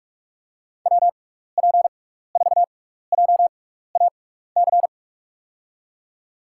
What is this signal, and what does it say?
Morse code, heard as a keyed tone, about 44 wpm, spaced as for 15 wpm, 700 Hz. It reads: WP4JAC
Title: USA Call Signs - 6 Characters - 6x Character Spacing - 45wpm